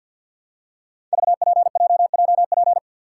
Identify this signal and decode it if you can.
Morse code, heard as a keyed tone, about 49 wpm, 700 Hz. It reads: VPJJP
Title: Code Groups - 5 Letters - 50wpm